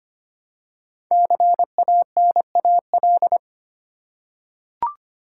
Morse code, heard as a keyed tone, about 25 wpm, 700 Hz. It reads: CANAL
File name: 5 Letter Words - First 3 Letters Given - 25wpm